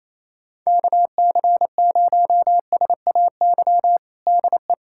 Morse code, heard as a keyed tone, about 28 wpm, 700 Hz. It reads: KC0SAY DE
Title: USA Callsign DE Callsign - 6 Characters - 28wpm